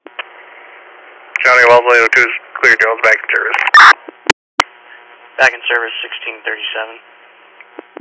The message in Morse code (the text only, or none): none